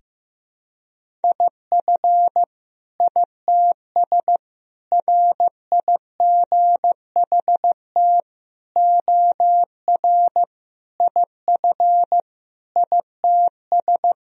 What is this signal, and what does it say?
Morse code, heard as a keyed tone, about 16 wpm, 700 Hz. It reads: IF ITS RIGHT OR IF ITS
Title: Sentences from Top 1000 Words - Review Part 3 - 15wpm